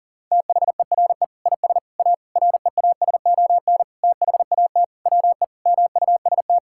Morse code, heard as a keyed tone, about 40 wpm, 700 Hz. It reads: THERE IS A REASON THAT WE MUST
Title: Sentences from Top 1000 Words - Review Part 5 - 40wpm